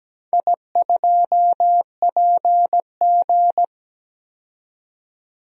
Morse code, heard as a keyed tone, about 17 wpm, 700 Hz. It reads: I2PG
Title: DXCC-100 Call Signs - 4 Characters - 17wpm